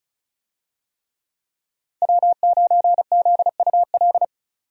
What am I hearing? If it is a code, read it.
Morse code, heard as a keyed tone, about 34 wpm, 700 Hz. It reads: W9ZUL
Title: USA Call Signs - 5 Characters - Repeated 3x - 35wpm